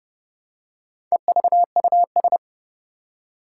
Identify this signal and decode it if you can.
Morse code, heard as a keyed tone, about 30 wpm, 700 Hz. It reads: EVUS
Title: Four Challenging Letter-Number Combinations 30wpm